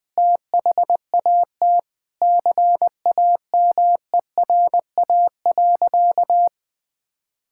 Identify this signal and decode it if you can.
Morse code, heard as a keyed tone, about 20 wpm, 700 Hz. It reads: THAT CAMERA.
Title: Sentences from Top 1000 Words - Review Part 5 - 20wpm